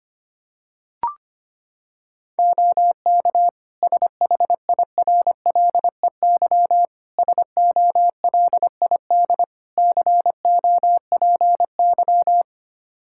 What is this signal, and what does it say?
Morse code, heard as a keyed tone, about 25 wpm, 700 Hz. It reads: OK SHIRLEY SOLID COPY
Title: Practice QSO - Exchange 3 - More Info - 25wpm